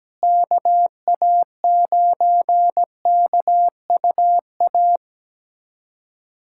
Morse code, heard as a keyed tone, about 17 wpm, 700 Hz. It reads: KA9KUA
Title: USA Callsign DE Callsign 17wpm